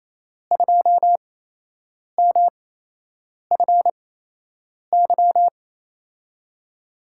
Morse code, heard as a keyed tone, about 28 wpm, spaced as for 8 wpm, 700 Hz. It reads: 2MFY
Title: USA Call Signs - 6 Characters - 8x Character Spacing - 28wpm